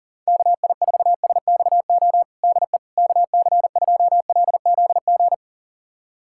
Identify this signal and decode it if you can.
Morse code, heard as a keyed tone, about 40 wpm, 700 Hz. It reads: KI4SXO DE KC2LZG
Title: USA Callsign DE Callsign 40wpm